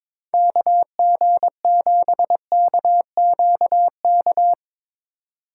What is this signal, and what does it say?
Morse code, heard as a keyed tone, about 21 wpm, 700 Hz. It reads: KG7KQK
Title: USA Call Signs - 6 Characters - 2x Character Spacing - 22wpm